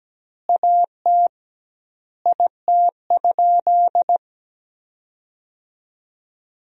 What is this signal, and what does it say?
Morse code, heard as a keyed tone, about 17 wpm, 700 Hz. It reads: AT IT?
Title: 2x Word-Spacing - Sentences from Top 100 Words - 17wpm